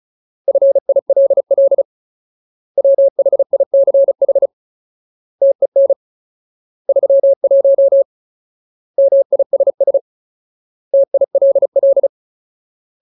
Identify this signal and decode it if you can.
Morse code, heard as a keyed tone, about 34 wpm, 550 Hz. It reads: FILL WHICH TEN 31 MISS TILL
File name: Ultimate Word List / 35WPM Ultimate